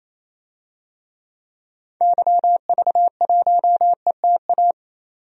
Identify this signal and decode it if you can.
Morse code, heard as a keyed tone, about 28 wpm, 700 Hz. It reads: YV1ETA
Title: DXCC-100 Call Signs - 6 Characters - 28wpm